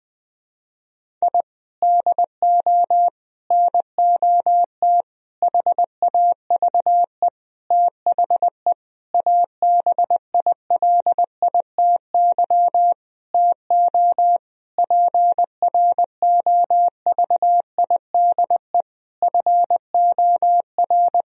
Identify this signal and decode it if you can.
Morse code, heard as a keyed tone, about 21 wpm, 700 Hz. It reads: I DO NOT HAVE THE ABILITY TO PROVIDE FOR
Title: Sentences from Top 1000 Words - Review Part 3 - 20wpm